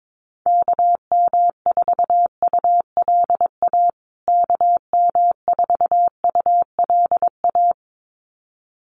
Morse code, heard as a keyed tone, about 22 wpm, 700 Hz. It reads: KM4ULA KM4ULA
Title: USA Call Signs Repeated 2x 22wpm